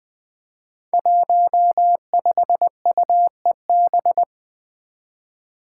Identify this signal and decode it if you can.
Morse code, heard as a keyed tone, about 20 wpm, 700 Hz. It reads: 15UEB
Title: Five Challenging Letter-Number Combinations 20wpm